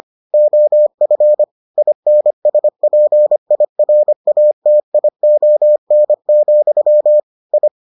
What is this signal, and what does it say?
Morse code, heard as a keyed tone, about 25 wpm, 600 Hz. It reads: OF INSPIRATION, I